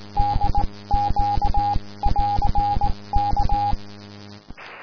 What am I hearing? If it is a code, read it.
Morse code, heard as a keyed tone, about 17 wpm, spaced as for 15 wpm, 820 Hz, with noise. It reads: DQ <AR> K
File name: ei4dq / aurora 15 march 2012 / DG5CST